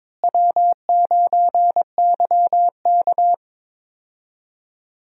Morse code, heard as a keyed tone, about 22 wpm, 700 Hz. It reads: W9YK
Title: USA Call Signs - 4 Characters - Repeated 3x - 22wpm